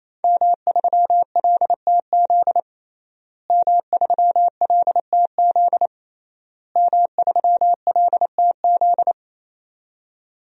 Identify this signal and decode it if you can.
Morse code, heard as a keyed tone, about 28 wpm, 700 Hz. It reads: M3LTZ M3LTZ M3LTZ
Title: DXCC-100 Call Signs - 5 Characters - Repeated 3x - 28wpm